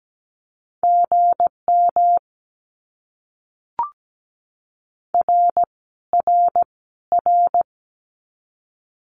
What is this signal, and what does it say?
Morse code, heard as a keyed tone, about 17 wpm, 700 Hz. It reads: GM R R R
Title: Abbreviations Repeated 3x - 17wpm